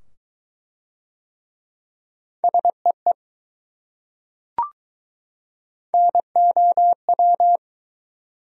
Morse code, heard as a keyed tone, about 20 wpm, 700 Hz. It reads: SEE NOW